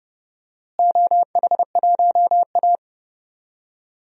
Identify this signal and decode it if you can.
Morse code, heard as a keyed tone, about 32 wpm, 700 Hz. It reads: OH1A
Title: DXCC-100 Call Signs - 4 Characters - Repeated 3x - 30wpm